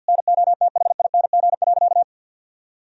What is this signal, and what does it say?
Morse code, heard as a keyed tone, about 48 wpm, 700 Hz. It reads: NOTHING.
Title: Sentences from Top 1000 Words - Review Part 2 - 50wpm